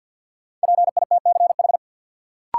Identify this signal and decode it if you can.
Morse code, heard as a keyed tone, about 53 wpm, 700 Hz. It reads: PITCH
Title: Sentences from Top 1300 Words - New Word Drill Alt - 50wpm